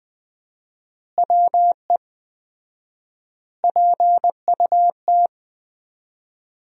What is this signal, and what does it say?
Morse code, heard as a keyed tone, about 20 wpm, 700 Hz. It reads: WE PUT
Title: 4x Word-Spacing - Sentences from Top 200 Words - 20wpm